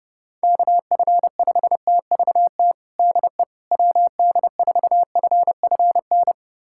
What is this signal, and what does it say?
Morse code, heard as a keyed tone, about 30 wpm, 700 Hz. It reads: KF5TVT DE WD4FFN
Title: USA Callsign DE Callsign - 6 Characters - 30wpm